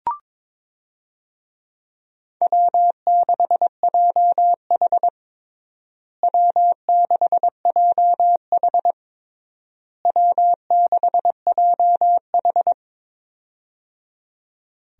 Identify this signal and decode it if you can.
Morse code, heard as a keyed tone, about 22 wpm, 700 Hz. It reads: W6JH W6JH W6JH
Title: USA Call Signs - 4 Characters - Repeated 3x - 22wpm